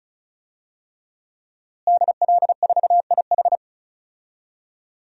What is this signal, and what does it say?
Morse code, heard as a keyed tone, about 36 wpm, 700 Hz. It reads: DL4IH
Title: DXCC-100 Call Signs - 5 Characters - Repeated 3x - 35wpm